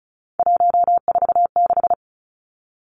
Morse code, heard as a keyed tone, about 34 wpm, 700 Hz. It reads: 146
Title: Numbers - 3-Digit 35wpm